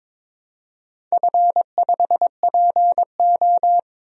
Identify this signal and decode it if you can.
Morse code, heard as a keyed tone, about 21 wpm, 700 Hz. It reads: F5PO
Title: DXCC-100 Call Signs - 4 Characters - 22wpm